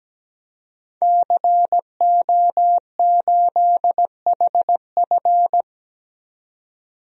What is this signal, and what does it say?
Morse code, heard as a keyed tone, about 17 wpm, 700 Hz. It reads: CO8HF
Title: DXCC-100 Call Signs - 5 Characters - 17wpm